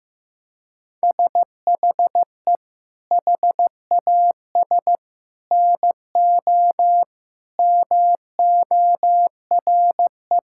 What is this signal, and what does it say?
Morse code, heard as a keyed tone, about 16 wpm, 700 Hz. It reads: SHE HAS NO MORE
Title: Sentences from Top 1000 Words - Review Part 6 - 15wpm